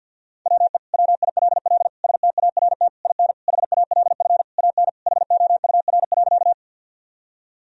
Morse code, heard as a keyed tone, about 52 wpm, 700 Hz. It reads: WE WILL START IN HALF AN HOUR.
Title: Sentences from Top 1000 Words - Review Part 1 - 50wpm